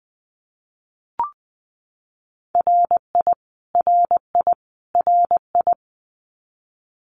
Morse code, heard as a keyed tone, about 20 wpm, 700 Hz. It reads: RI RI RI
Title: US States Repeated 3x 20wpm